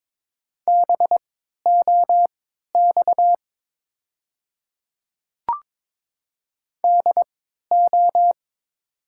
Morse code, heard as a keyed tone, about 22 wpm, spaced as for 13 wpm, 700 Hz. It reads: BOX DO
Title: 3x Character Spacing - Top 500 Words - 22wpm